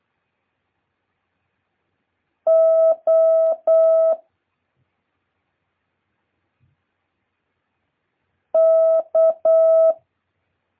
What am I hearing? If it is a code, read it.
Morse code, heard as a keyed tone, about 8 wpm, 650 Hz. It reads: O K
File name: jh8cbh / 2023cw